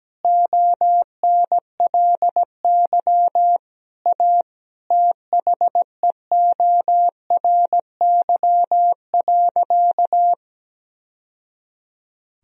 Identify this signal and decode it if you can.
Morse code, heard as a keyed tone, about 17 wpm, 700 Hz. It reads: ONLY A THEORY.
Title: Sentences from Top 1000 Words - Review Part 3 - 17wpm